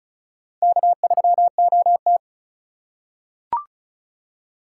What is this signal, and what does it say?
Morse code, heard as a keyed tone, about 36 wpm, 700 Hz. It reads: K3OT
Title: USA Call Signs - 4 Characters - 3x Character Spacing - 35wpm